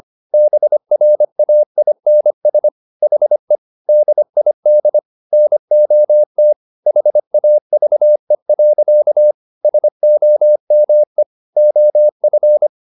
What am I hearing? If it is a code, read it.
Morse code, heard as a keyed tone, about 25 wpm, 600 Hz. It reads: BRAINS HE DID NOT HAVE. SOME OF
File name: Never 0025wpm